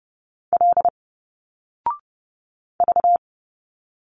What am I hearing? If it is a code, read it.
Morse code, heard as a keyed tone, about 30 wpm, 700 Hz. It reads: L V
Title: Single Letters 30wpm